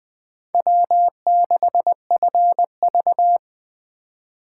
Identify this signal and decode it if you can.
Morse code, heard as a keyed tone, about 20 wpm, 700 Hz. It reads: W6FV
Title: Speed Racing - USA 1x2 Call Signs 17wpm